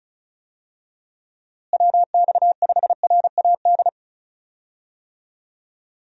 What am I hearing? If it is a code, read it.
Morse code, heard as a keyed tone, about 34 wpm, 700 Hz. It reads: WX5RAD
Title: USA Call Signs - 6 Characters - Repeated 3x - 35wpm